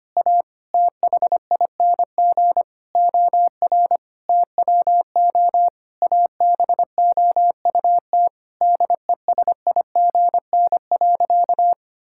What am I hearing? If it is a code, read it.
Morse code, heard as a keyed tone, about 25 wpm, 700 Hz. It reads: A THING OR TWO ABOUT DESIGN.
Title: Sentences from Top 1000 Words - Review Part 5 - 25wpm